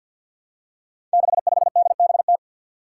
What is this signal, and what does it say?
Morse code, heard as a keyed tone, about 51 wpm, 700 Hz. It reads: 65DBT
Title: Five Challenging Letter-Number Combinations 50wpm